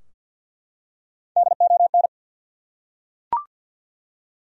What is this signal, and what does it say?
Morse code, heard as a keyed tone, about 50 wpm, 700 Hz. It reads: DON
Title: USA Common Names - 50wpm